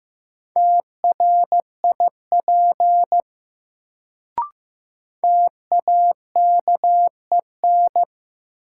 Morse code, heard as a keyed tone, about 15 wpm, 700 Hz. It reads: TRIP TAKEN
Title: Sentences from Top 1000 Words - Review Part 4 - New Word Drill - 15wpm